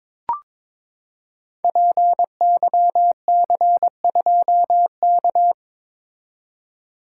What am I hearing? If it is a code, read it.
Morse code, heard as a keyed tone, about 23 wpm, 700 Hz. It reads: PYC2K
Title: Code Groups - 5 Letter:Numbers - 22wpm